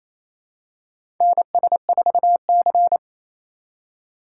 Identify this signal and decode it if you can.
Morse code, heard as a keyed tone, about 28 wpm, 700 Hz. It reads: NS4C
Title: USA Call Signs - 4 Characters - 2x Character Spacing - 28wpm